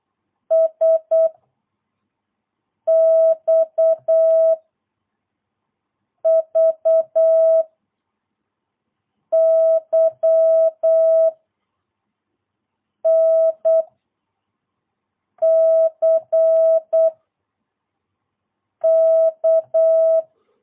Morse code, heard as a keyed tone, about 8 wpm, 650 Hz. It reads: S X V Y N C K
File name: jh8cbh / 2022cw